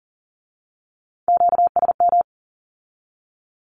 Morse code, heard as a keyed tone, about 40 wpm, 700 Hz. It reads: QSM
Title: Q-Codes with Spoken Meaning - 40wpm